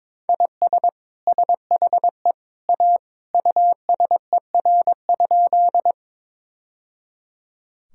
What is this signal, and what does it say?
Morse code, heard as a keyed tone, about 21 wpm, 700 Hz. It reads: IS SHE A USER?
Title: Sentences from Top 1000 Words - Review Part 5 - 22wpm